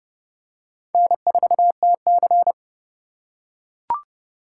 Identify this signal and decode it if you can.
Morse code, heard as a keyed tone, about 29 wpm, 700 Hz. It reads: N4TC